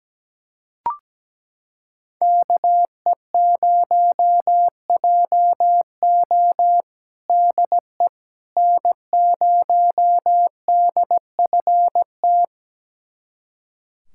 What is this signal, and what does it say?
Morse code, heard as a keyed tone, about 17 wpm, 700 Hz. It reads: KE0JO DE N0DFT
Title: USA Callsign DE Callsign - 5 Characters - 17wpm